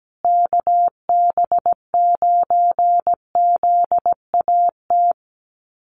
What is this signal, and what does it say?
Morse code, heard as a keyed tone, about 17 wpm, 700 Hz. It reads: KB9ZAT
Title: USA Call Signs Repeated 3x 17wpm